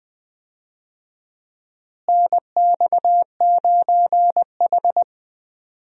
Morse code, heard as a keyed tone, about 21 wpm, 700 Hz. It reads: NX9H
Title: USA Call Signs - 4 Characters - Repeated 3x - 20wpm